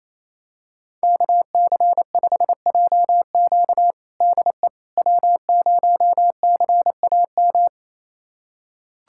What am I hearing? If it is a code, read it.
Morse code, heard as a keyed tone, about 28 wpm, 700 Hz. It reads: KC5JQ DE W0CAM
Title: USA Callsign DE Callsign - 5 Characters - 28wpm